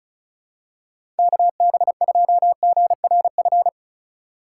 Morse code, heard as a keyed tone, about 34 wpm, 700 Hz. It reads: KB2GRF
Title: USA Call Signs - 6 Characters - 2x Character Spacing - 35wpm